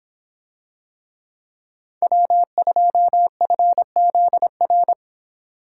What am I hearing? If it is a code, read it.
Morse code, heard as a keyed tone, about 26 wpm, 700 Hz. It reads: W2FZR